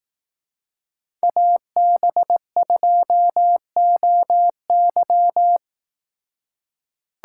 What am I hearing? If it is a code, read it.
Morse code, heard as a keyed tone, about 18 wpm, 700 Hz. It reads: AB2OY
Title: Speed Racing - USA 2x2 Call Signs 15wpm